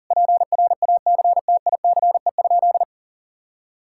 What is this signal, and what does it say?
Morse code, heard as a keyed tone, about 40 wpm, 700 Hz. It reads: PRACTICE?
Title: Sentences from Top 1000 Words - Review Part 5 - 40wpm